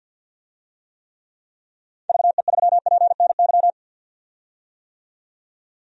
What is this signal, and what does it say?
Morse code, heard as a keyed tone, about 52 wpm, 700 Hz. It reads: VE3PNY
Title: DXCC-100 Call Signs - 6 Characters - 50wpm